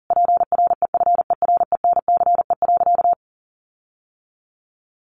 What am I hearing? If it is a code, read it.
Morse code, heard as a keyed tone, about 40 wpm, 700 Hz. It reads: PREFERENCE.
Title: Sentences from Top 1500 Words 40wpm